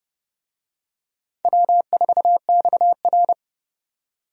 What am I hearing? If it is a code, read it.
Morse code, heard as a keyed tone, about 30 wpm, 700 Hz. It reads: W4XR
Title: USA Call Signs - 4 Characters - 30wpm